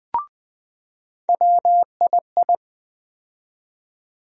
Morse code, heard as a keyed tone, about 20 wpm, 700 Hz. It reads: WII
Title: USA Call Sign Postfixes - 3 Letters 20wpm